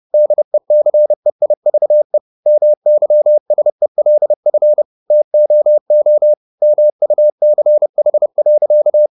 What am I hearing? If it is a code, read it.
Morse code, heard as a keyed tone, about 30 wpm, 600 Hz. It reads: DECEIVE MYSELF TOO MUCH.